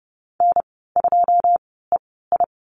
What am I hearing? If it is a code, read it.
Morse code, heard as a keyed tone, about 30 wpm, spaced as for 17 wpm, 700 Hz. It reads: N2EI
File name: Lesson 14g - Number 2 - Call Signs 30wpm